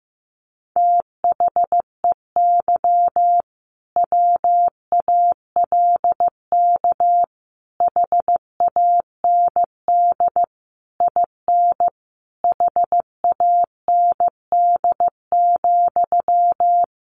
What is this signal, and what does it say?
Morse code, heard as a keyed tone, about 15 wpm, 700 Hz. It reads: THEY WALK HAND IN HAND,
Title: Sentences from Top 1400 Words 15wpm